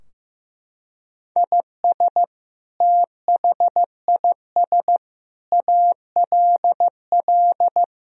Morse code, heard as a keyed tone, about 15 wpm, 700 Hz. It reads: IS THIS ALL
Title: Sentences from Top 1000 Words - Review Part 3 - 15wpm